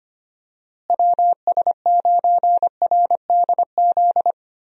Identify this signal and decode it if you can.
Morse code, heard as a keyed tone, about 25 wpm, 700 Hz. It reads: WS9RDZ
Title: USA Call Signs - 6 Characters - Repeated 2x - 25wpm